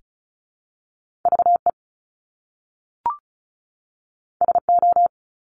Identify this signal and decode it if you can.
Morse code, heard as a keyed tone, about 36 wpm, 700 Hz. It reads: VE SO
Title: Top Two Letter Combinations in Words 35wpm